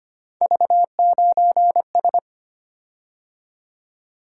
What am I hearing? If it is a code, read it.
Morse code, heard as a keyed tone, about 25 wpm, 700 Hz. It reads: V9S
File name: Three Letter-Number Combinations - 25wpm